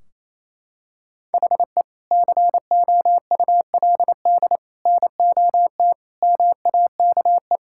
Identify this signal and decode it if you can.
Morse code, heard as a keyed tone, about 28 wpm, 700 Hz. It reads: HE COULD NOT MAKE
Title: Sentences from Top 1600 Words 28wpm